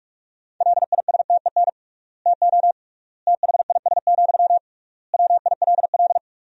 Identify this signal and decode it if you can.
Morse code, heard as a keyed tone, about 46 wpm, 700 Hz. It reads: LISTEN TO THIS, WILL
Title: 3x Word-Spacing - Sentences from Top 400 Words - 45wpm